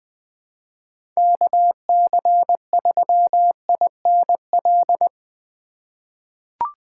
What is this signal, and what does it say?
Morse code, heard as a keyed tone, about 20 wpm, 700 Hz. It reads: KC3INL
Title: USA Call Signs - 6 Characters - 3x Character Spacing - 20wpm